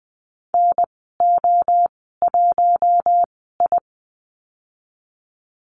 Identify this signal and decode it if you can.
Morse code, heard as a keyed tone, about 20 wpm, spaced as for 14 wpm, 700 Hz. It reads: NO1I
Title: 048 Lesson 9g - Number 4 - Call Signs 20wpm